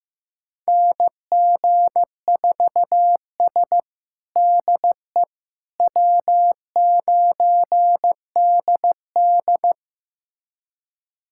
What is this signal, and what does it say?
Morse code, heard as a keyed tone, about 15 wpm, 700 Hz. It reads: NG4S DE W9DD
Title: USA Callsign DE Callsign - 4 Characters - 15wpm